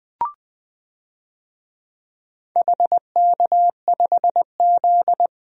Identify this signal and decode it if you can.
Morse code, heard as a keyed tone, about 20 wpm, 700 Hz. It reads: HK5Z
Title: DXCC-100 Call Signs - 4 Characters - 20wpm